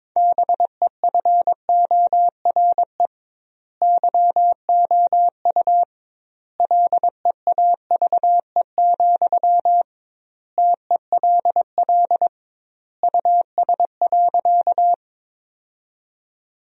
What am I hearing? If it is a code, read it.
Morse code, heard as a keyed tone, about 22 wpm, 700 Hz. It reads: BEFORE YOU LEAVE, TELL US.
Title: 2x Word-Spacing - Sentences from Top 400 Words - 22wpm